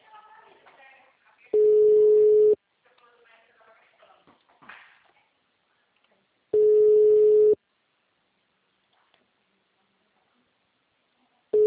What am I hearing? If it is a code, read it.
Ringback tone (Europe, 425 Hz).
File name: out- -2005- - - .415